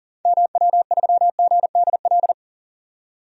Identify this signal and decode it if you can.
Morse code, heard as a keyed tone, about 40 wpm, 700 Hz. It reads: MW3GDL
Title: DXCC-100 Call Signs - 6 Characters - 40wpm